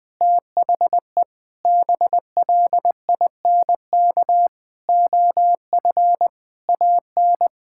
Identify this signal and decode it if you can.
Morse code, heard as a keyed tone, about 20 wpm, 700 Hz. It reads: THE BLINK OF AN
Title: Sentences with 6-Word Phrases - 20wpm